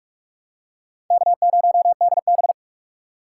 Morse code, heard as a keyed tone, about 44 wpm, 700 Hz. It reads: K0DB
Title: USA Call Signs - 4 Characters - 3x Character Spacing - 45wpm